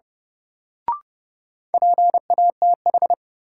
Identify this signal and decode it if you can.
Morse code, heard as a keyed tone, about 30 wpm, 700 Hz. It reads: PATH
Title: Sentences from Top 1400 Words - New Word Drill - 30wpm